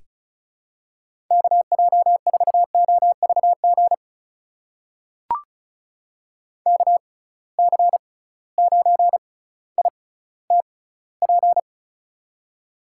Morse code, heard as a keyed tone, about 36 wpm, spaced as for 18 wpm, 700 Hz. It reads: KJ4OVG KC9ITP
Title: USA Call Signs - 6 Characters - 6x Character Spacing - 35wpm